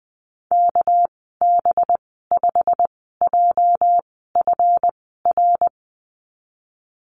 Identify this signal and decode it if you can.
Morse code, heard as a keyed tone, about 20 wpm, spaced as for 14 wpm, 700 Hz. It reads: KB5JFR
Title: 238 Lesson 38g - Letter q - Call Signs 20wpm